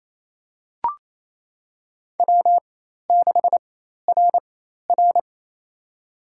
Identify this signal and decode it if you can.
Morse code, heard as a keyed tone, about 28 wpm, spaced as for 13 wpm, 700 Hz. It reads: W6RR
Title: USA Call Signs - 4 Characters - 4x Character Spacing - 28wpm